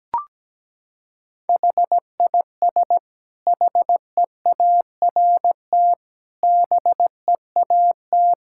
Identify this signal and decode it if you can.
Morse code, heard as a keyed tone, about 17 wpm, 700 Hz. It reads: HIS HEART BEAT
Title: Sentences from Top 1600 Words 17wpm